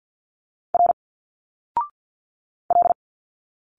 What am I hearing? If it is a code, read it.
Morse code, heard as a keyed tone, about 51 wpm, 700 Hz. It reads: R L
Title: Single Letters 50wpm